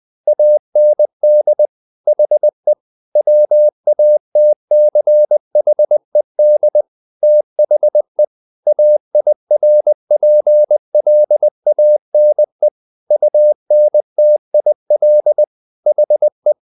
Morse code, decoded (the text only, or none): AND HE WATCHED THE AIRPLANE UNTIL HE